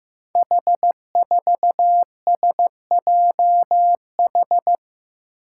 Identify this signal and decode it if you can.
Morse code, heard as a keyed tone, about 15 wpm, 700 Hz. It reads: H4SJH
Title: Five Challenging Letter-Number Combinations 15wpm